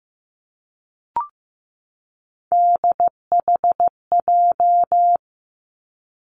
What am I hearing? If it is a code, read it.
Morse code, heard as a keyed tone, about 15 wpm, 700 Hz. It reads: DHJ
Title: Three Challenging Letter-Number Combinations 15wpm